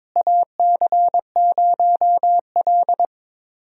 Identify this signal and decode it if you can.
Morse code, heard as a keyed tone, about 22 wpm, 700 Hz. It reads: AC0L
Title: USA Call Signs - 4 Characters - Repeated 3x - 22wpm